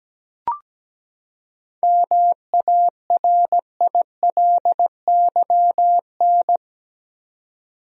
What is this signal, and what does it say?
Morse code, heard as a keyed tone, about 17 wpm, 700 Hz. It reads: MARILYN
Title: USA Common Names - 17wpm